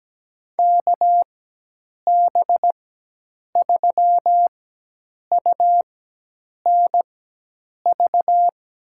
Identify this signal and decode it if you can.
Morse code, heard as a keyed tone, about 17 wpm, spaced as for 8 wpm, 700 Hz. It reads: KB3UNV
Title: USA Call Signs - 6 Characters - 4x Character Spacing - 17wpm